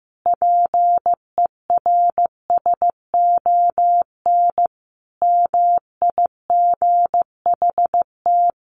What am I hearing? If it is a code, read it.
Morse code, heard as a keyed tone, about 15 wpm, 700 Hz. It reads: PERSON MIGHT
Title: Sentences from Top 900 Words 15wpm